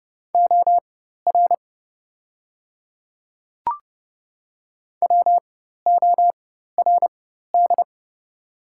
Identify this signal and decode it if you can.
Morse code, heard as a keyed tone, about 30 wpm, spaced as for 14 wpm, 700 Hz. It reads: OR WORD
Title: 4x Character Spacing - Top 300 Words - 30wpm